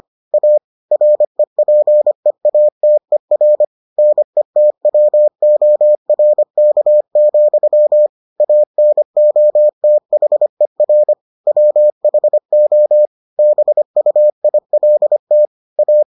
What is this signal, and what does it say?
Morse code, heard as a keyed tone, about 25 wpm, 600 Hz. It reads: A REPEATER NETWORK, ANOTHER WHO BUILT A